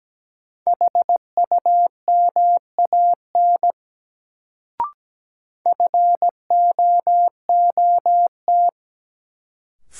Morse code, heard as a keyed tone, about 17 wpm, 700 Hz. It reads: HUMAN FOOT
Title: Sets of 5 Words - New Word Drill - 17wpm